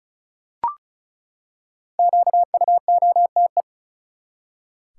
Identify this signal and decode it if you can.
Morse code, heard as a keyed tone, about 36 wpm, 700 Hz. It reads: QUOTE
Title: Five Letter Words - 35wpm